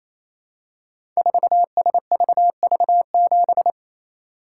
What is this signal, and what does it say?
Morse code, heard as a keyed tone, about 27 wpm, 700 Hz. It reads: 4SVV7
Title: Code Groups - 5 Letter:Numbers - 28wpm